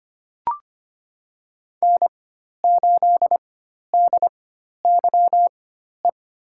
Morse code, heard as a keyed tone, about 25 wpm, spaced as for 12 wpm, 700 Hz. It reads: N8DYE
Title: USA Call Signs - 5 Characters - 4x Character Spacing - 25wpm